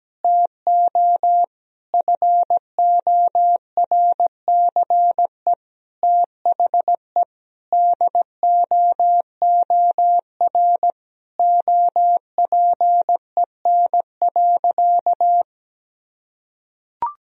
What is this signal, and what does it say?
Morse code, heard as a keyed tone, about 17 wpm, 700 Hz. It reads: TO FORCE THE DOOR OPEN.
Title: Sentences from Top 1000 Words - Review Part 3 - 17wpm